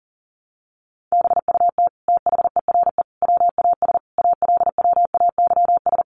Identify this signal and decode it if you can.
Morse code, heard as a keyed tone, about 40 wpm, 700 Hz. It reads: BUT THERE WAS ALWAYS
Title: Sentences from Top 900 Words 40wpm